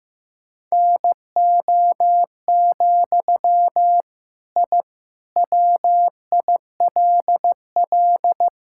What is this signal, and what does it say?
Morse code, heard as a keyed tone, about 16 wpm, 700 Hz. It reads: NO, I WILL
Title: Sentences from Top 1000 Words - Review Part 5 - 15wpm